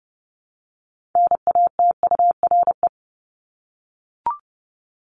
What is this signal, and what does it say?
Morse code, heard as a keyed tone, about 30 wpm, 700 Hz. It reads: NATURE
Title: Top 501-750 Words 30wpm